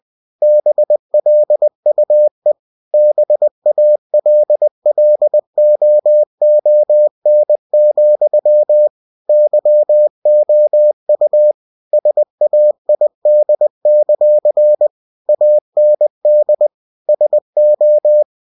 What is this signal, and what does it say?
Morse code, heard as a keyed tone, about 20 wpm, 600 Hz. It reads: BLUE BALLOON, YOU SAID; AND SO
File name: cw-books / winnie-the-pooh / winnie-20wpm-page-0006